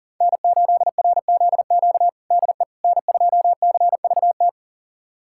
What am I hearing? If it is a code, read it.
Morse code, heard as a keyed tone, about 40 wpm, 700 Hz. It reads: N8RZQ DE N2CVT
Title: USA Callsign DE Callsign - 5 Characters - 40wpm